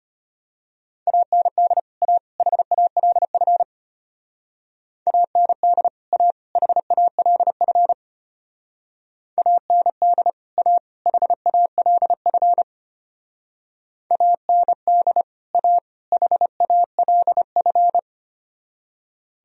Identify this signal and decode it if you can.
Morse code, heard as a keyed tone, about 32 wpm, 700 Hz. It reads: AND A HALF AND A HALF AND A HALF AND A HALF